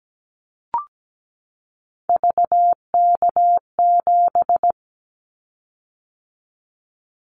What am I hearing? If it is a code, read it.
Morse code, heard as a keyed tone, about 17 wpm, 700 Hz. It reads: VK7
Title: Callsign Prefixes for Top 100 DXCC - 17wpm